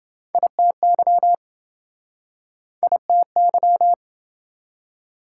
Morse code, heard as a keyed tone, about 26 wpm, 700 Hz. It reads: ITY ITY